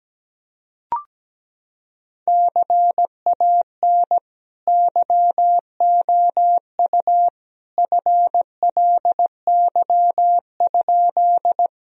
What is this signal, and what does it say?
Morse code, heard as a keyed tone, about 17 wpm, 700 Hz. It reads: CAN YOU FLY?
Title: Sentences from Top 1000 Words - Review Part 2 - 17wpm